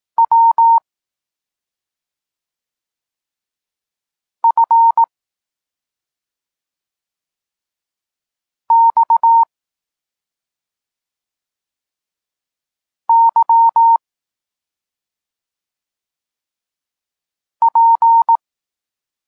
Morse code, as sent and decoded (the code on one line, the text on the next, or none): .-- / ..-. / -..- / -.-- / .--.
W F X Y P